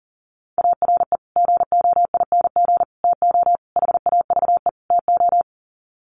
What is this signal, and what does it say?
Morse code, heard as a keyed tone, about 40 wpm, 700 Hz. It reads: ARE GOING TO HAVE TO
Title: Sets of 5 Words 40wpm